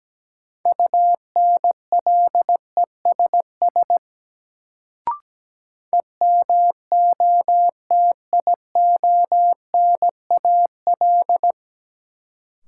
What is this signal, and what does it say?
Morse code, heard as a keyed tone, about 17 wpm, 700 Hz. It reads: UNLESS EMOTIONAL
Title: Sentences from Top 1000 Words - Review Part 5 - New Word Drill - 17wpm